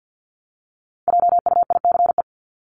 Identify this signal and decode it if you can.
Morse code, heard as a keyed tone, about 50 wpm, 700 Hz. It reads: JUICE
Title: Top 1251-1500 Words 50wpm